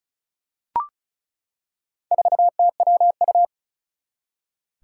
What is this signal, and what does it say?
Morse code, heard as a keyed tone, about 36 wpm, 700 Hz. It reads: 4TWU
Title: Four Challenging Letter-Number Combinations 35wpm